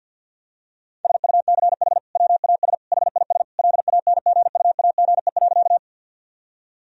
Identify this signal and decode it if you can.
Morse code, heard as a keyed tone, about 50 wpm, 700 Hz. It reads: SUCH WAS HIS LANGUAGE.
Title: Sentences from Top 1000 Words - Review Part 3 - 50wpm